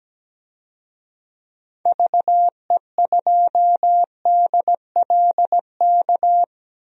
Morse code, heard as a keyed tone, about 17 wpm, 700 Hz. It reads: VE2DLK
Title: DXCC-100 Call Signs - 6 Characters - 17wpm